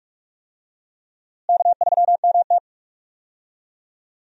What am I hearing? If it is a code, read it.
Morse code, heard as a keyed tone, about 46 wpm, 700 Hz. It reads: K3MT
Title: USA Call Signs - 4 Characters - Repeated 2x - 45wpm